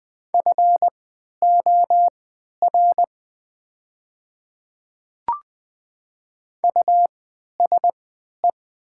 Morse code, heard as a keyed tone, about 20 wpm, spaced as for 11 wpm, 700 Hz. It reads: FOR USE
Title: 3x Character Spacing - Top 100 Words - 20wpm